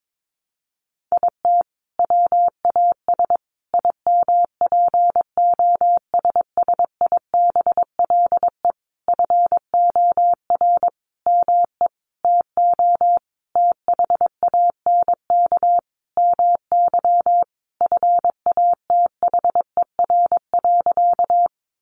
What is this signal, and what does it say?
Morse code, heard as a keyed tone, about 22 wpm, 700 Hz. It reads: IT WAS IMPOSSIBLE FOR ME TO THANK MY FATHER.
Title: Sentences from Top 1100 Words 22wpm